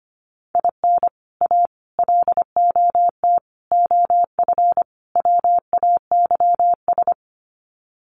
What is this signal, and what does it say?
Morse code, heard as a keyed tone, about 25 wpm, 700 Hz. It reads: IN A LOT OF WAYS
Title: Sets of 5 Words Encore 25wpm